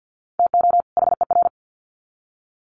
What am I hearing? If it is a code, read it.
Morse code, heard as a keyed tone, about 50 wpm, 700 Hz. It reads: TO HER
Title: Sets of 2 Words 50wpm